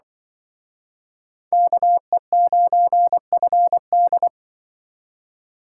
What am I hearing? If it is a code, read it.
Morse code, heard as a keyed tone, about 23 wpm, 700 Hz. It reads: KE9FD
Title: Speed Racing - USA 2x2 Call Signs 20wpm